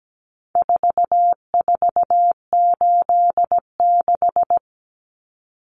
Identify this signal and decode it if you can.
Morse code, heard as a keyed tone, about 17 wpm, 700 Hz. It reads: 4486
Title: Numbers - 4-Digit 17wpm